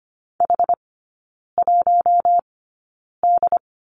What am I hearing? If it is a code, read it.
Morse code, heard as a keyed tone, about 25 wpm, spaced as for 9 wpm, 700 Hz. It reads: H1D
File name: course / 25:10wpm / 093 Lesson 16e - Letter c - Three Characters 25wpm